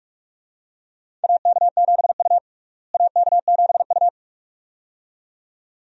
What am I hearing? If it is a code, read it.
Morse code, heard as a keyed tone, about 46 wpm, 700 Hz. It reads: AK7U AK7U
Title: USA Call Signs - 4 Characters - Repeated 2x - 45wpm